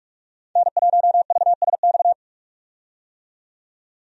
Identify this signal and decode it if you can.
Morse code, heard as a keyed tone, about 46 wpm, 700 Hz. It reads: N1VSX
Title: USA Call Signs - 5 Characters - 45wpm